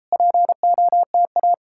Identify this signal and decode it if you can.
Morse code, heard as a keyed tone, about 40 wpm, 700 Hz. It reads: POTA